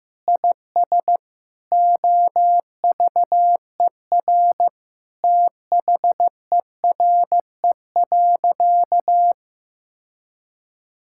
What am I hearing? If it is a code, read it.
Morse code, heard as a keyed tone, about 15 wpm, 700 Hz. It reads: IS OVER THERE.
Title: Sentences from Top 1600 Words 15wpm